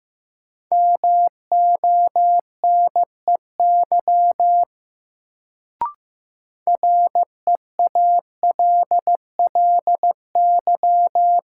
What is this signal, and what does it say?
Morse code, heard as a keyed tone, about 15 wpm, 700 Hz. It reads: MONEY REALLY
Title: Sets of 4 Words - New Word Drill - 15wpm